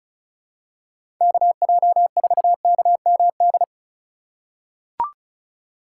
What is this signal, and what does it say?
Morse code, heard as a keyed tone, about 36 wpm, 700 Hz. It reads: KJ4KMD
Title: USA Call Signs - 4x Character Spacing - 35wpm